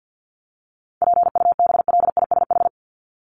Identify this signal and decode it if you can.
Morse code, heard as a keyed tone, about 50 wpm, 700 Hz. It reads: PUBLISH
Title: ED Suffix - Top 100 Words with Base Words 50wpm